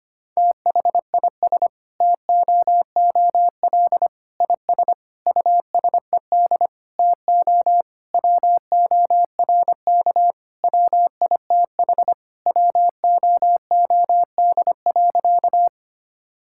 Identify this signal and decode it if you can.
Morse code, heard as a keyed tone, about 25 wpm, 700 Hz. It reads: THIS TOOL IS USED TO WORK WITH WOOD.
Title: Sentences from Top 1000 Words - Review Part 6 - 25wpm